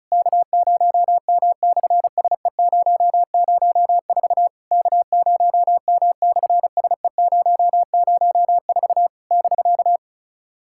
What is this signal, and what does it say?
Morse code, heard as a keyed tone, about 36 wpm, 700 Hz. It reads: K0M/SE004 K0M/SE004 <BK>
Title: SOTA QSO Practice 35wpm